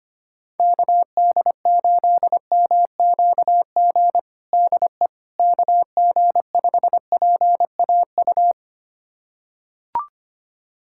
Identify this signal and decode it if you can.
Morse code, heard as a keyed tone, about 25 wpm, 700 Hz. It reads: KD8MQG DE KG5PAU
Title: USA Callsign DE Callsign 25wpm